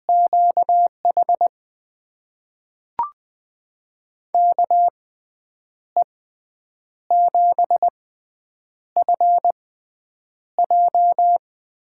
Morse code, heard as a keyed tone, about 20 wpm, spaced as for 8 wpm, 700 Hz. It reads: QH KE7FJ
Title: USA Call Signs - 6 Characters - 6x Character Spacing - 20wpm